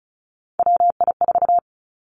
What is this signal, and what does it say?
Morse code, heard as a keyed tone, about 36 wpm, 700 Hz. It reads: WI4
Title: Three Challenging Letter-Number Combinations 35wpm